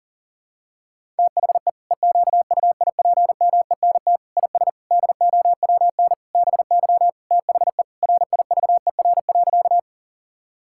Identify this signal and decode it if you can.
Morse code, heard as a keyed tone, about 40 wpm, 700 Hz. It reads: THE EQUIPMENT IS DOWN BY THE RIVER.
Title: Sentences from Top 1000 Words - Review Part 5 - 40wpm